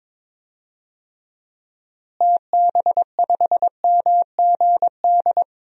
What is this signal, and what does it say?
Morse code, heard as a keyed tone, about 21 wpm, 700 Hz. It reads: TB5MGD
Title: DXCC-100 Call Signs - 6 Characters - Repeated 2x - 22wpm